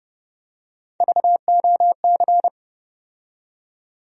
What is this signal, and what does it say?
Morse code, heard as a keyed tone, about 30 wpm, 700 Hz. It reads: VOC
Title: USA Call Sign Postfixes - 3 Letters 30wpm